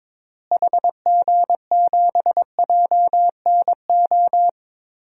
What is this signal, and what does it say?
Morse code, heard as a keyed tone, about 22 wpm, 700 Hz. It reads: HG7JNO
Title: DXCC-100 Call Signs - 6 Characters - Repeated 2x - 22wpm